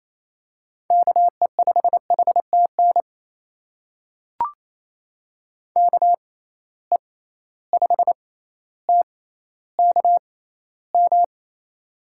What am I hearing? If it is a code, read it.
Morse code, heard as a keyed tone, about 28 wpm, spaced as for 14 wpm, 700 Hz. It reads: KE5HTN KE5TKM
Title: USA Call Signs - 6 Characters - 6x Character Spacing - 28wpm